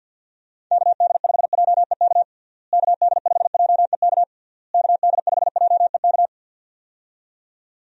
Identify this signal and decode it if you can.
Morse code, heard as a keyed tone, about 52 wpm, 700 Hz. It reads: KD5JEK KD5JEK KD5JEK
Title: USA Call Signs - 6 Characters - Repeated 3x - 50wpm